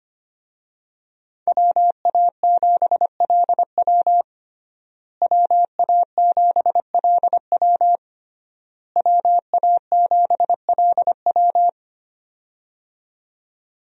Morse code, heard as a keyed tone, about 25 wpm, 700 Hz. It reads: WA7LW WA7LW WA7LW
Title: USA Call Signs - 5 Characters - Repeated 3x - 25wpm